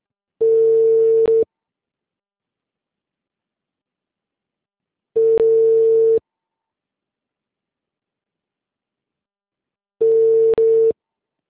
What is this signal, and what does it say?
Ringback tone (China, 450 Hz).
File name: out- -2017